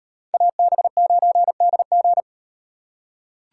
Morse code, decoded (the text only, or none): AB9DG